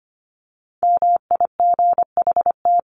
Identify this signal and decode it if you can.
Morse code, heard as a keyed tone, about 25 wpm, 700 Hz. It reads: MIGHT
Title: Top 300 Words 25wpm